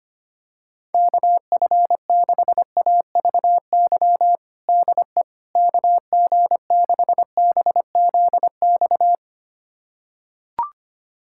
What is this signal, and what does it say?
Morse code, heard as a keyed tone, about 25 wpm, 700 Hz. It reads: KF6AVY DE KG6BZX
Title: USA Callsign DE Callsign - 6 Characters - 25wpm